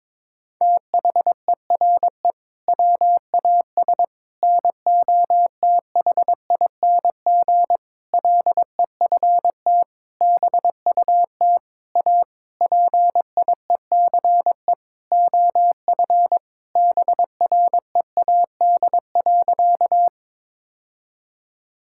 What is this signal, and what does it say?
Morse code, heard as a keyed tone, about 22 wpm, 700 Hz. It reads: THERE WAS NOTHING LEFT BUT A PIECE OF BREAD.
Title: Sentences from Top 1600 Words 22wpm